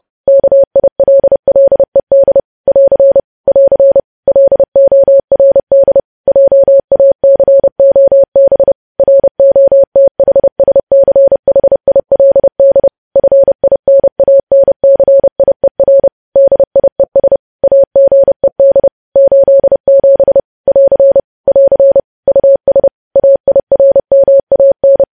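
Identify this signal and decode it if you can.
Morse code, heard as a keyed tone, about 30 wpm, 550 Hz. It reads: KILLED <AR> <AR> LORD JACOB ROTHSCHILD FINANCIER DIES AGED 87 <AR> <AR> US AIRMAN
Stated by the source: News headlines sent in Morse at 30 WPM.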